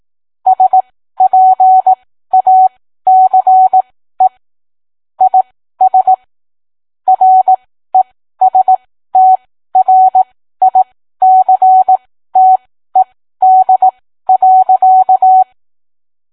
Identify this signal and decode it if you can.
Morse code, heard as a keyed tone, about 18 wpm, spaced as for 13 wpm, 750 Hz. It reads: SPACE IS RESTRICTED.